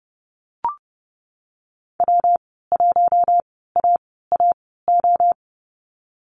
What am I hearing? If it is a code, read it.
Morse code, heard as a keyed tone, about 30 wpm, spaced as for 17 wpm, 700 Hz. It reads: W1AAO
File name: Lesson 19g - Letter w - Call Signs 30wpm